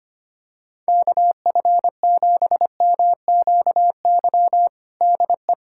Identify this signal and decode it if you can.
Morse code, heard as a keyed tone, about 25 wpm, 700 Hz. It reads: KF7MQY DE
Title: USA Callsign DE Callsign - 6 Characters - 25wpm